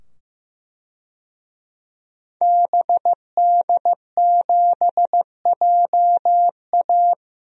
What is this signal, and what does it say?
Morse code, heard as a keyed tone, about 15 wpm, 700 Hz. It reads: BD7JA
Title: DXCC-100 Call Signs - 5 Characters - 15wpm